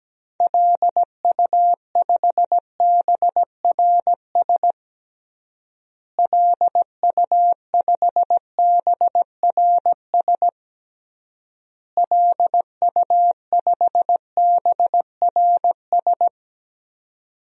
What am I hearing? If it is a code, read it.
Morse code, heard as a keyed tone, about 17 wpm, 700 Hz. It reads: LU5BRS LU5BRS LU5BRS
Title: DXCC-100 Call Signs - 6 Characters - Repeated 3x - 17wpm